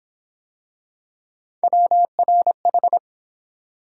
Morse code, heard as a keyed tone, about 28 wpm, 700 Hz. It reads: WRH